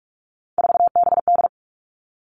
Morse code, heard as a keyed tone, about 46 wpm, 700 Hz. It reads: 4BD
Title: Three Challenging Letter-Number Combinations 45wpm